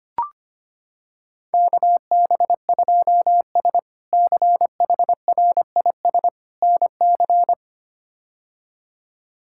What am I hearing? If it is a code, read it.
Morse code, heard as a keyed tone, about 25 wpm, 700 Hz. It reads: KB2S CHRIS NC
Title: K1USN SST QSO Practice - 25wpm